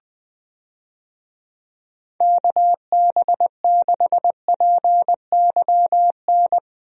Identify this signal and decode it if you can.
Morse code, heard as a keyed tone, about 20 wpm, 700 Hz. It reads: KB6PYN
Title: USA Call Signs - 6 Characters - Repeated 2x - 20wpm